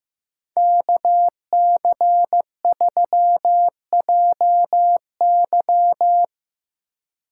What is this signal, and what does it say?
Morse code, heard as a keyed tone, about 17 wpm, 700 Hz. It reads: KC3JY
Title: Speed Racing - USA 2x2 Call Signs 15wpm